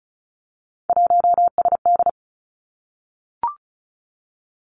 Morse code, heard as a keyed tone, about 36 wpm, 700 Hz. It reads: 1SD
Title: Three Challenging Letter-Number Combinations 35wpm